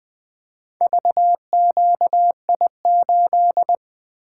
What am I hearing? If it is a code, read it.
Morse code, heard as a keyed tone, about 20 wpm, 700 Hz. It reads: VQI8
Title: Four Letter-Number Combinations - 20wpm